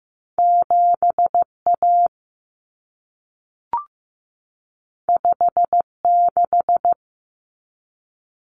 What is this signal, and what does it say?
Morse code, heard as a keyed tone, about 15 wpm, 700 Hz. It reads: 7A 56
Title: Common Cut Numbers - 2-Digit 15wpm